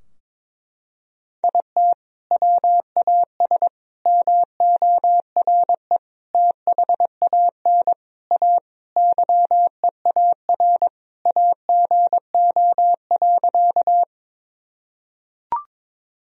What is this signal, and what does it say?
Morse code, heard as a keyed tone, about 21 wpm, 700 Hz. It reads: IT WAS MORE THAN A YEAR AGO.
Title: Sentences from Top 1000 Words - Review Part 3 - 22wpm